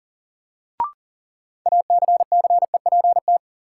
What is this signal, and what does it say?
Morse code, heard as a keyed tone, about 38 wpm, 700 Hz. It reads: ACCEPT
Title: Sentences from Top 1300 Words - New Word Drill - 40wpm